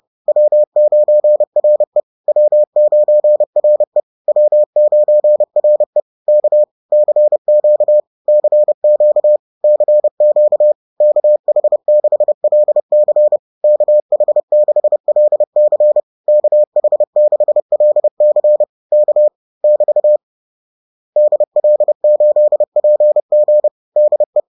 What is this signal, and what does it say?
Morse code, heard as a keyed tone, about 30 wpm, 600 Hz. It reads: W9RE W9RE W9RE K CQ CQ CQ KH6LC KH6LC KH6LC K <BT> DL8PG DE
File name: 30wpm-4